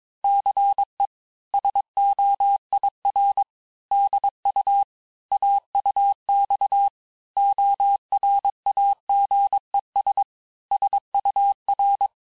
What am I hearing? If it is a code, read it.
Morse code, heard as a keyed tone, about 22 wpm, 800 Hz. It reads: CE SOIR DU AUX ORAGES SUR
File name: cours-du-mercredi / audios / cours-nr-860-FIN-de-QSOr-vitesse-21MPM